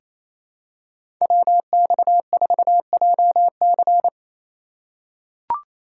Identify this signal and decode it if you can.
Morse code, heard as a keyed tone, about 28 wpm, 700 Hz. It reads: WX4JC
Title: USA Call Signs - 5 Characters - 4x Character Spacing - 28wpm